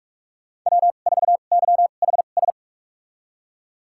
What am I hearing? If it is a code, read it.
Morse code, heard as a keyed tone, about 44 wpm, spaced as for 32 wpm, 700 Hz. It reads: W4YHS
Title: USA Call Signs - 2x Character Spacing - 45wpm